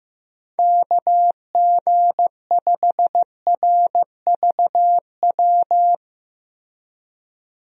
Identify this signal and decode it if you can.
Morse code, heard as a keyed tone, about 16 wpm, 700 Hz. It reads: KG5RVW
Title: USA Call Signs - 6 Characters - 15wpm